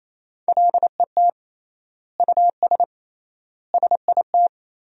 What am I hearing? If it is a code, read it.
Morse code, heard as a keyed tone, about 28 wpm, 700 Hz. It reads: LET US SIT
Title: 3x Word-Spacing - Sentences from Top 500 Words - 28wpm